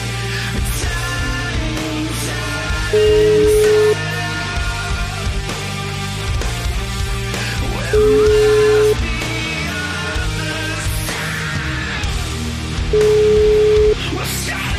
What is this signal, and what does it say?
Ringback tone (Europe, 425 Hz), with noise.